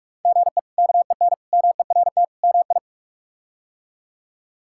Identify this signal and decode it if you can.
Morse code, heard as a keyed tone, about 46 wpm, 700 Hz. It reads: GE KEN MERT MI
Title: K1USN SST QSO Practice - 45wpm